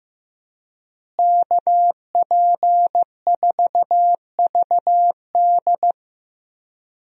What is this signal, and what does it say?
Morse code, heard as a keyed tone, about 15 wpm, 700 Hz. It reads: KP4VD
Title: DXCC-100 Call Signs - 5 Characters - Repeated 3x - 15wpm